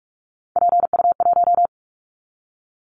Morse code, heard as a keyed tone, about 46 wpm, 700 Hz. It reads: PU1
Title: Callsign Prefixes for Top 50 DXCC - 45wpm